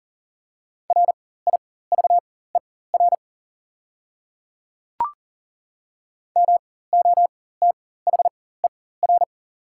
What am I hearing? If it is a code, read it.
Morse code, heard as a keyed tone, about 40 wpm, spaced as for 19 wpm, 700 Hz. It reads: RIVER MOTHER
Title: 4x Character Spacing - Top 300 Words - 40wpm